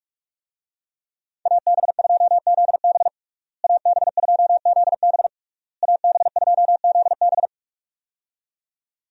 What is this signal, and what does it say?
Morse code, heard as a keyed tone, about 42 wpm, 700 Hz. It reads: AB2ZB AB2ZB AB2ZB
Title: USA Call Signs - 5 Characters - Repeated 3x - 45wpm